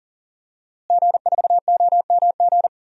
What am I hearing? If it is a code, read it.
Morse code, heard as a keyed tone, about 39 wpm, 700 Hz. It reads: G4OMG
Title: DXCC-100 Call Signs - 5 Characters - Repeated 3x - 40wpm